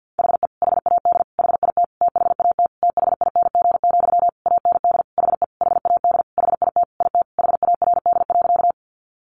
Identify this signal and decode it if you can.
Morse code, heard as a keyed tone, about 50 wpm, 700 Hz. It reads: HE HAD HIT THAT THING, AND HE HAD HIT IT HARD.
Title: Sentences from Top 1200 Words 50wpm